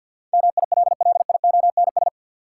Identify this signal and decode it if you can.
Morse code, heard as a keyed tone, about 53 wpm, 700 Hz. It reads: MILLIONS
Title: Sets of 5 Words - New Word Drill - 50wpm